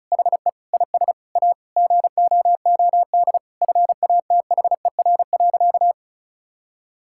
Morse code, heard as a keyed tone, about 36 wpm, 700 Hz. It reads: HE IS A GOOD FATHER.
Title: Sentences from Top 1000 Words - Review Part 5 - 35wpm